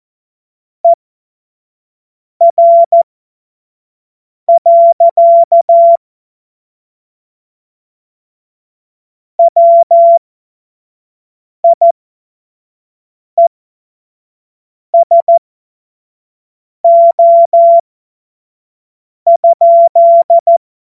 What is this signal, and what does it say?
Morse code, heard as a keyed tone, about 14 wpm, spaced as for 5 wpm, 680 Hz. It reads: ER. WIESO?
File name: Morsekurs / Witze / Witz 15. November 2022 25 F70